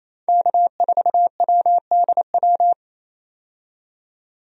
Morse code, heard as a keyed tone, about 28 wpm, 700 Hz. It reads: K4WDW
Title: USA Call Signs - 5 Characters - 28wpm